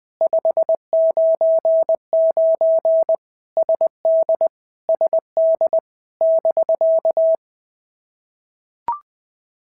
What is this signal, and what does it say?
Morse code, heard as a keyed tone, about 20 wpm, 650 Hz. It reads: 599 SD SD <BK>
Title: POTA QSO Practice - 20wpm